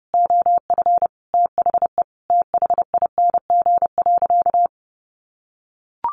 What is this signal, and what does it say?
Morse code, heard as a keyed tone, about 28 wpm, 700 Hz. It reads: OF THE THING.
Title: Sentences from Top 1300 Words 30wpm